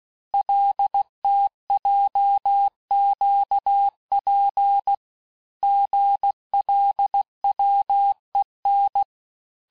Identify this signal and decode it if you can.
Morse code, heard as a keyed tone, about 16 wpm, 800 Hz. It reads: LTJQP GLWEN
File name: cours-du-lundi files / Cours-1586-texte-chiffre-vitesse-15mpm